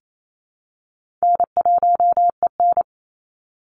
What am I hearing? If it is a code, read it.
Morse code, heard as a keyed tone, about 28 wpm, 700 Hz. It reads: N1EN
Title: Call Signs Repeated 2x - 28wpm